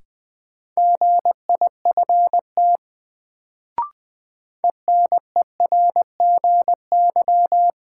Morse code, heard as a keyed tone, about 21 wpm, 700 Hz. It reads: GIFT ENERGY
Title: Sentences from Top 900 Words - New Word Drill - 20wpm